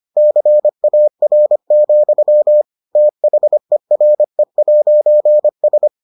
Morse code, heard as a keyed tone, about 25 wpm, 600 Hz. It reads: CAR, THERE'S